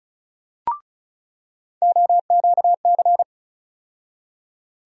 Morse code, heard as a keyed tone, about 34 wpm, 700 Hz. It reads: OQC
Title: USA Call Sign Postfixes - 3 Letters 35wpm